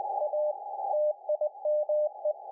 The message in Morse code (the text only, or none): A TIME